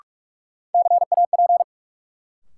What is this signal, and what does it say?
Morse code, heard as a keyed tone, about 43 wpm, 700 Hz. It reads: CAP
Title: Sentences from Top 1600 Words - New Word Drill Alt - 45wpm